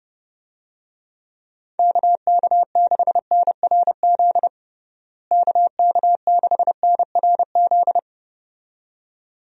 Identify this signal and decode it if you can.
Morse code, heard as a keyed tone, about 30 wpm, 700 Hz. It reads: KK6NRZ KK6NRZ
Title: USA Call Signs - 6 Characters - Repeated 2x - 30wpm